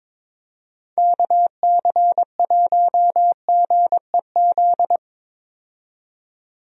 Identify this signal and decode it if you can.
Morse code, heard as a keyed tone, about 22 wpm, 700 Hz. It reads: KC1GEZ
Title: USA Call Signs - 6 Characters - 22wpm